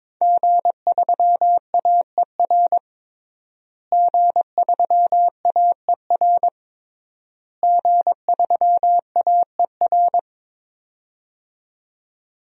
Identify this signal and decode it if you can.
Morse code, heard as a keyed tone, about 22 wpm, 700 Hz. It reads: G3AER G3AER G3AER
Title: DXCC-100 Call Signs - 5 Characters - Repeated 3x - 22wpm